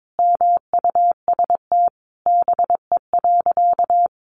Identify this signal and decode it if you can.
Morse code, heard as a keyed tone, about 22 wpm, 700 Hz. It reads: MUST BE.
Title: Sentences from Top 1400 Words 22wpm